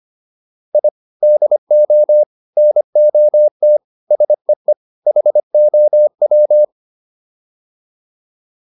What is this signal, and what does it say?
Morse code, heard as a keyed tone, about 25 wpm, 600 Hz. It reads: I DO NOT SEE HOW